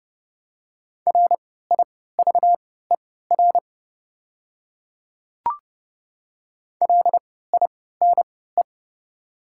Morse code, heard as a keyed tone, about 30 wpm, spaced as for 17 wpm, 700 Hz. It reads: RIVER LINE
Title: 3x Character Spacing - Top 500 Words - 30wpm